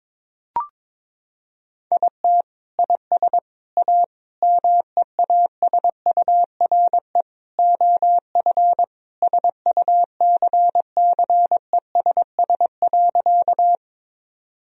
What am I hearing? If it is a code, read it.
Morse code, heard as a keyed tone, about 21 wpm, 700 Hz. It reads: IT IS A MEASURE OF SUCCESS.
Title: Sentences from Top 1000 Words - Review Part 2 - 22wpm